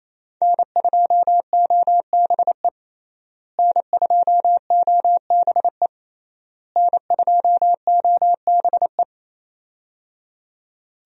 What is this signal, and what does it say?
Morse code, heard as a keyed tone, about 28 wpm, 700 Hz. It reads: N2OBE N2OBE N2OBE
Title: USA Call Signs - 5 Characters - Repeated 3x - 28wpm